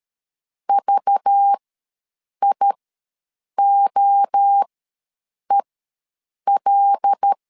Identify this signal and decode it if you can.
Morse code, heard as a keyed tone, about 13 wpm, spaced as for 7 wpm, 800 Hz. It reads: VIOEL
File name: LEZIONI / 5 LBVC / 5 LBV ws 12wpm 3015